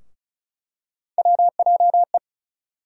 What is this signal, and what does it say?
Morse code, heard as a keyed tone, about 36 wpm, 700 Hz. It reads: WJE
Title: USA Call Sign Postfixes - 3 Letters 35wpm